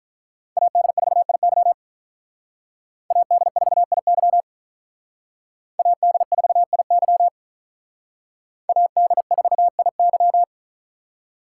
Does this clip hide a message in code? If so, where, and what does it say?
Yes: Morse code from 0.6 to 10.4 s, heard as a keyed tone, about 45 wpm, 700 Hz. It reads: AD4IY AD4IY AD4IY AD4IY